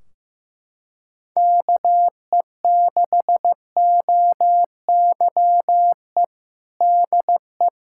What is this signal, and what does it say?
Morse code, heard as a keyed tone, about 15 wpm, 700 Hz. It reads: KE6OYE DE
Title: Practice QSO - Exchange 4 - Ending - 15wpm